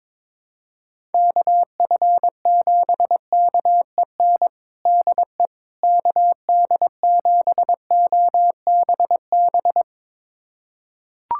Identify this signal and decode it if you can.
Morse code, heard as a keyed tone, about 22 wpm, 700 Hz. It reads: KF7KEN DE KD7OBB
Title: USA Callsign DE Callsign - 6 Characters - 22wpm